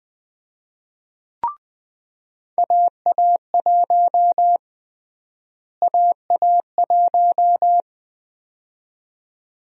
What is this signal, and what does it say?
Morse code, heard as a keyed tone, about 20 wpm, 700 Hz. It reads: AA1 AA1
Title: USA Call Sign Prefixes - Repeated 2x - 20wpm